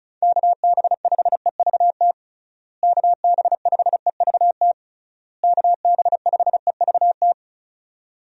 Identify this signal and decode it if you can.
Morse code, heard as a keyed tone, about 36 wpm, 700 Hz. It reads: KB5EVT KB5EVT KB5EVT
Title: USA Call Signs - 6 Characters - Repeated 3x - 35wpm